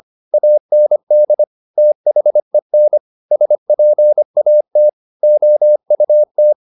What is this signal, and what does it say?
Morse code, heard as a keyed tone, about 25 wpm, 600 Hz. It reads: AND THEN SPAT OUT
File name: the old man and the sea part 1 25wpm